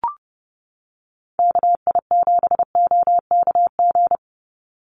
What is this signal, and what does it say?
Morse code, heard as a keyed tone, about 30 wpm, 700 Hz. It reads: KI7OKG
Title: USA Call Signs 30wpm